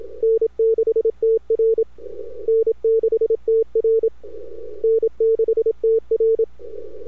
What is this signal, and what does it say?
Morse code, heard as a keyed tone, about 27 wpm, 450 Hz. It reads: N6TR N6TR N6TR
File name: n6tr / PTTThump